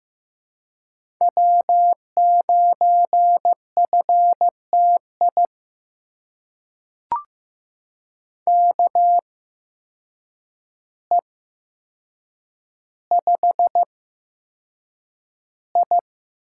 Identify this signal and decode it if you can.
Morse code, heard as a keyed tone, about 15 wpm, spaced as for 7 wpm, 700 Hz. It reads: W9FTI KE5I
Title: USA Call Signs - 8x Character Spacing - 15wpm